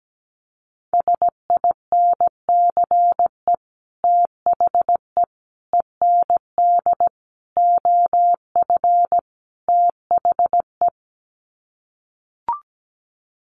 Morse code, heard as a keyed tone, about 17 wpm, 700 Hz. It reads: SINCE THE END OF THE
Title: Sets of 5 Words Encore 17wpm